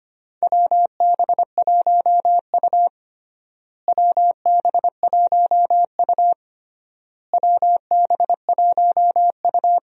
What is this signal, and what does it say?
Morse code, heard as a keyed tone, about 25 wpm, 700 Hz. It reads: WB1U WB1U WB1U
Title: USA Call Signs - 4 Characters - Repeated 3x - 25wpm